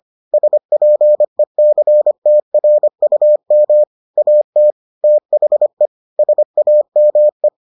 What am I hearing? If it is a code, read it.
Morse code, heard as a keyed tone, about 25 wpm, 600 Hz. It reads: SPECTRUM AT THE SAME